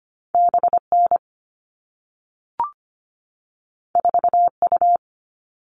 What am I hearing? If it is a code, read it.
Morse code, heard as a keyed tone, about 25 wpm, 700 Hz. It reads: B N 4 U
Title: All Cut Numbers - 2-Digit 25wpm